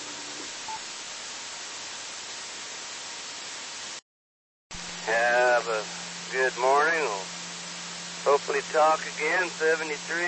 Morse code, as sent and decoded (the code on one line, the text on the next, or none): none